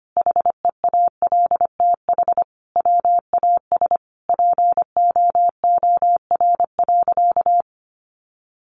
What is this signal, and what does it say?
Morse code, heard as a keyed tone, about 25 wpm, 700 Hz. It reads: HEALTH WAS POOR.
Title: Sentences from Top 1000 Words 25wpm